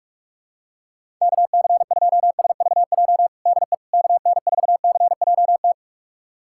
Morse code, heard as a keyed tone, about 41 wpm, 700 Hz. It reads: KC2SVJ DE KN4CJT
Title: USA Callsign DE Callsign - 6 Characters - 45wpm